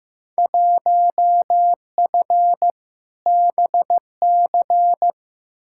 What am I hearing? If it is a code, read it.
Morse code, heard as a keyed tone, about 15 wpm, 700 Hz. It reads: 1F BC
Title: Field Day Exchanges Encore - 15wpm